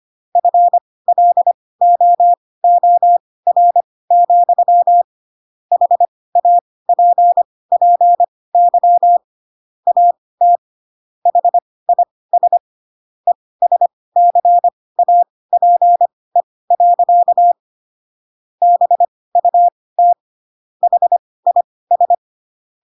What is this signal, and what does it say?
Morse code, heard as a keyed tone, about 25 wpm, spaced as for 18 wpm, 700 Hz. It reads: FLOOR, HAPPY AT HIS ESCAPE. BUT HIS
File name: Pinocchio / Pinocchio ch4 12 25 ik1hge